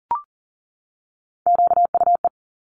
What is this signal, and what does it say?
Morse code, heard as a keyed tone, about 40 wpm, 700 Hz. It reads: QUE
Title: Top Three Letter Combinations in Words 40wpm